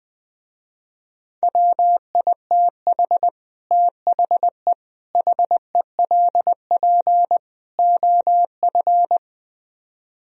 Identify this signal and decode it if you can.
Morse code, heard as a keyed tone, about 21 wpm, 700 Hz. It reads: WITH THE HELP OF